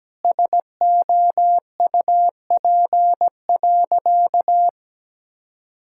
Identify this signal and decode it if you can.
Morse code, heard as a keyed tone, about 17 wpm, 700 Hz. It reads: SOUP.
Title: Sentences from Top 1000 Words - Review Part 6 - 17wpm